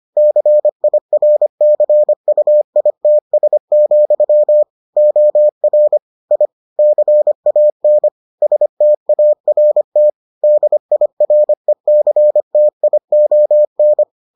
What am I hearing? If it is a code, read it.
Morse code, heard as a keyed tone, about 25 wpm, 600 Hz. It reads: CIRCUITS, OR I CAN START DIRECTION